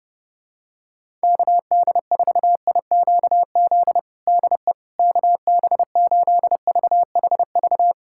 Morse code, heard as a keyed tone, about 30 wpm, 700 Hz. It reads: KD4IQZ DE KB8VHV
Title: USA Callsign DE Callsign - 6 Characters - 30wpm